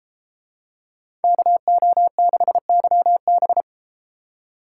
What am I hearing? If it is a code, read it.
Morse code, heard as a keyed tone, about 33 wpm, 700 Hz. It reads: KO6YB